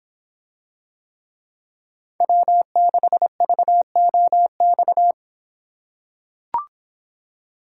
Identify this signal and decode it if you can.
Morse code, heard as a keyed tone, about 25 wpm, 700 Hz. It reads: W6VOX